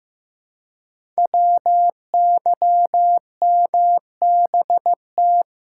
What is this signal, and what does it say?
Morse code, heard as a keyed tone, about 15 wpm, 700 Hz. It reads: WYMBT
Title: Code Groups - 5 Letters - 15wpm